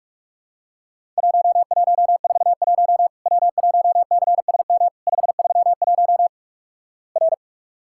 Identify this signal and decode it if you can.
Morse code, heard as a keyed tone, about 48 wpm, 700 Hz. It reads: 1141 W1CSM 531 R
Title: Worked All Europe DX Contest - QTC Exchange - 45wpm